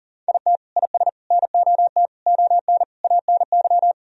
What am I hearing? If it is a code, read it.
Morse code, heard as a keyed tone, about 40 wpm, 700 Hz. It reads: IT IS NOT ON ANY
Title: Sentences from Top 1000 Words - Review Part 2 - 40wpm